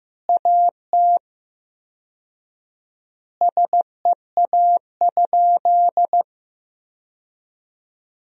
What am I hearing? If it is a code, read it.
Morse code, heard as a keyed tone, about 15 wpm, 700 Hz. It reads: AT SEA?
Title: 4x Word-Spacing - Sentences from Top 300 Words - 15wpm